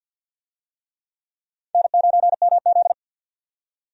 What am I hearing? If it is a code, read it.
Morse code, heard as a keyed tone, about 47 wpm, 700 Hz. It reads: N9MZ
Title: USA Call Signs - 4 Characters - 50wpm